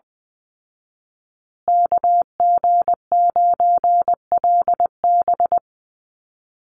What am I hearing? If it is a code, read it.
Morse code, heard as a keyed tone, about 20 wpm, 700 Hz. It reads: KG9LB
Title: Call Signs Repeated 3x - 20wpm